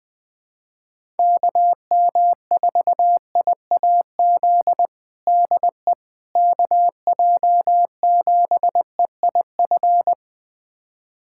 Morse code, heard as a keyed tone, about 20 wpm, 700 Hz. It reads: KM4IAZ DE KJ7EIF
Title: USA Callsign DE Callsign 20wpm